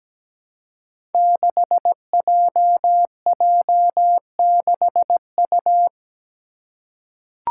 Morse code, heard as a keyed tone, about 17 wpm, 700 Hz. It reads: 6JJ6U
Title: Five Challenging Letter-Number Combinations 17wpm